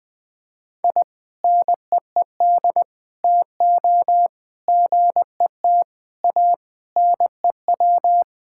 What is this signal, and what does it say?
Morse code, heard as a keyed tone, about 20 wpm, 700 Hz. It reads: I NEED TO GET A NEW
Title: Sentences with 3-Word Phrases - 20wpm